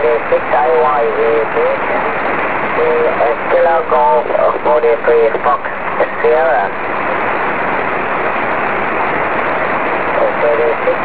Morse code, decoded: none